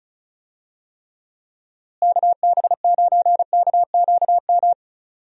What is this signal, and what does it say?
Morse code, heard as a keyed tone, about 34 wpm, 700 Hz. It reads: KB9KQM
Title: USA Call Signs - 6 Characters - Repeated 2x - 35wpm